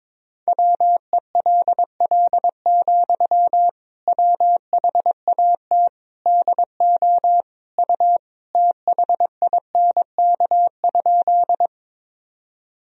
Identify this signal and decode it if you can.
Morse code, heard as a keyed tone, about 22 wpm, 700 Hz. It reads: WELL, WHAT DO U THINK?
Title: Sentences from Top 200 Words and Abbreviations 22wpm